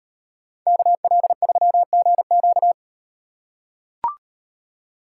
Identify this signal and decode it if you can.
Morse code, heard as a keyed tone, about 38 wpm, 700 Hz. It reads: KL3GQ